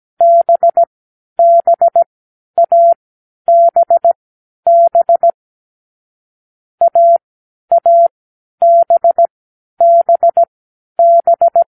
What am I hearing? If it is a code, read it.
Morse code, heard as a keyed tone, about 17 wpm, spaced as for 10 wpm, 680 Hz. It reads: BBABB AABBB